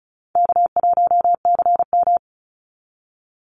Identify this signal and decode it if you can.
Morse code, heard as a keyed tone, about 34 wpm, 700 Hz. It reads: K1CM
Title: NA SOTA Chaser Callsigns 35wpm